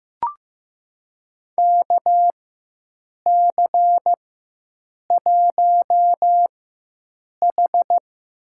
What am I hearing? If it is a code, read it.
Morse code, heard as a keyed tone, about 15 wpm, spaced as for 7 wpm, 700 Hz. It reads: KC1H
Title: USA Call Signs - 4 Characters - 4x Character Spacing - 15wpm